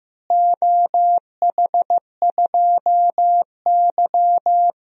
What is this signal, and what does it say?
Morse code, heard as a keyed tone, about 15 wpm, 700 Hz. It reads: OH2Y
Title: DXCC-100 Call Signs - 4 Characters - Repeated 2x - 15wpm